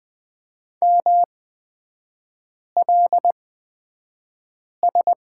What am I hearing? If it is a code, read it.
Morse code, heard as a keyed tone, about 20 wpm, spaced as for 5 wpm, 690 Hz. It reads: MLS
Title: 229 Lesson 37e - Letter x - Three Characters 20wpm